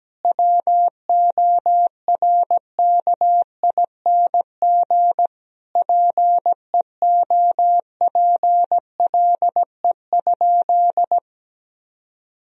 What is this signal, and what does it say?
Morse code, heard as a keyed tone, about 17 wpm, 700 Hz. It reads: WORKING PEOPLE?
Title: Sentences from Top 1000 Words - Review Part 4 - 17wpm